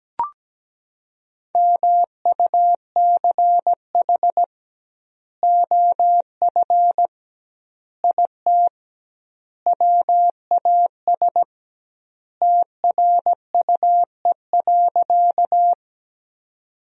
Morse code, heard as a keyed tone, about 17 wpm, 700 Hz. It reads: MUCH OF IT WAS TRUE.
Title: 2x Word-Spacing - Sentences from Top 400 Words - 17wpm